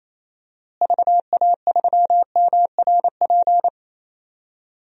Morse code, heard as a keyed tone, about 28 wpm, 700 Hz. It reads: VA3MRP
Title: DXCC-100 Call Signs - 6 Characters - Repeated 2x - 28wpm